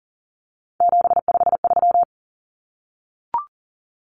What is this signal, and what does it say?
Morse code, heard as a keyed tone, about 40 wpm, 700 Hz. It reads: 753
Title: Common Cut Numbers - 3-Digit 40wpm